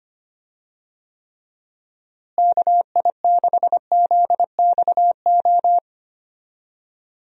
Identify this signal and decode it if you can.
Morse code, heard as a keyed tone, about 25 wpm, 700 Hz. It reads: KI6ZXO
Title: USA Call Signs - 6 Characters - Repeated 3x - 25wpm